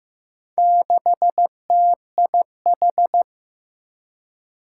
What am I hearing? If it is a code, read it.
Morse code, heard as a keyed tone, about 15 wpm, 700 Hz. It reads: 6TIH
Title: Four Challenging Letter-Number Combinations 15wpm